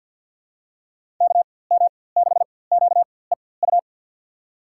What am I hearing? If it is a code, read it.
Morse code, heard as a keyed tone, about 53 wpm, spaced as for 24 wpm, 700 Hz. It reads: KM6QEU
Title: USA Call Signs - 6 Characters - 4x Character Spacing - 50wpm